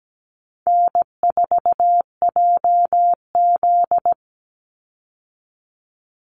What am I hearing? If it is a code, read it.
Morse code, heard as a keyed tone, about 17 wpm, 700 Hz. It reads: N4JZ
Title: USA Call Signs 17wpm